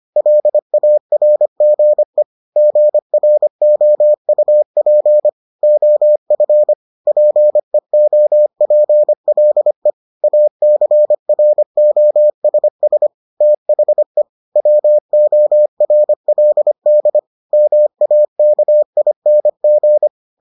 Morse code, heard as a keyed tone, about 25 wpm, 600 Hz. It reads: LARGE GROUP OF PEOPLE ACROSS THE WORLD MAKING